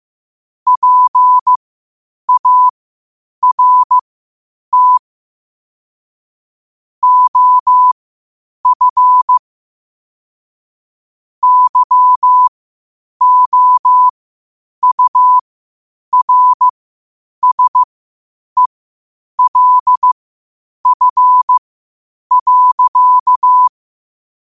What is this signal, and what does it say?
Morse code, heard as a keyed tone, about 15 wpm, spaced as for 8 wpm, 1000 Hz. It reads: PART OF YOURSELF.
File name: qotdfiles / 8wpm